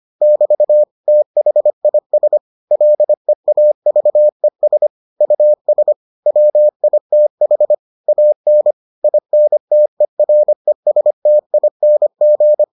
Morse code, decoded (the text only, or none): <BT> THIS LEAVES US WITH AN INTERESTING